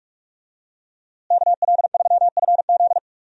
Morse code, heard as a keyed tone, about 42 wpm, 700 Hz. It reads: KL3FZ
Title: USA Call Signs - 5 Characters - Repeated 3x - 45wpm